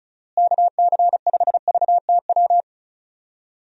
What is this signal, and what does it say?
Morse code, heard as a keyed tone, about 34 wpm, 700 Hz. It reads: KC5VTW
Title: USA Call Signs - 6 Characters - Repeated 2x - 35wpm